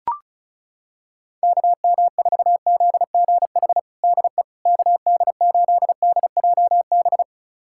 Morse code, heard as a keyed tone, about 34 wpm, 700 Hz. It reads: KM4ZGH DE KD8DJB
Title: USA Callsign DE Callsign 35wpm